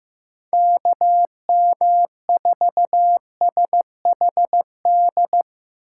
Morse code, heard as a keyed tone, about 15 wpm, 700 Hz. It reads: KM4SHD
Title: USA Call Signs - 6 Characters - 8x Character Spacing - 15wpm